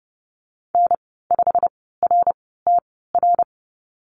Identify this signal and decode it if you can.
Morse code, heard as a keyed tone, about 30 wpm, spaced as for 17 wpm, 700 Hz. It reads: N5RTR
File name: Lesson 19g - Letter w - Call Signs 30wpm